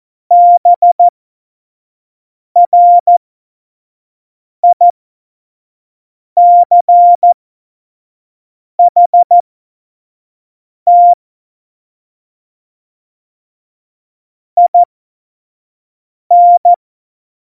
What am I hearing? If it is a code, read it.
Morse code, heard as a keyed tone, about 14 wpm, spaced as for 5 wpm, 700 Hz. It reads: BRICHT IN
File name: Morsekurs / Witze / Witz 24 April 2018, 25 F70